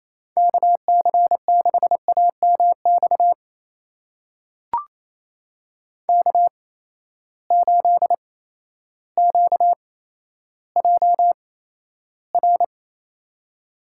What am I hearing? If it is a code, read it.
Morse code, heard as a keyed tone, about 28 wpm, spaced as for 13 wpm, 700 Hz. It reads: KC6AMX K8QJR
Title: USA Call Signs - 8x Character Spacing - 28wpm